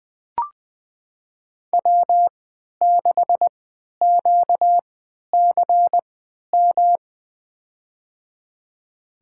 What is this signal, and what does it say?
Morse code, heard as a keyed tone, about 21 wpm, spaced as for 12 wpm, 700 Hz. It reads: W6QCM
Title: USA Call Signs - 5 Characters - 3x Character Spacing - 20wpm